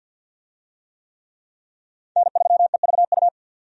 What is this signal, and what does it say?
Morse code, heard as a keyed tone, about 52 wpm, 700 Hz. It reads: N3EVU
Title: USA Call Signs - 5 Characters - 50wpm